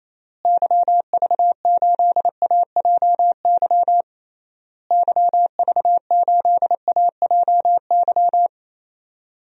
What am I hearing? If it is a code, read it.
Morse code, heard as a keyed tone, about 28 wpm, 700 Hz. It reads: YV8AJY YV8AJY
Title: DXCC-100 Call Signs - 6 Characters - Repeated 3x - 28wpm